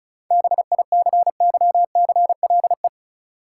Morse code, heard as a keyed tone, about 34 wpm, 700 Hz. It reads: BICYCLE
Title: Seven Letter Words - 35wpm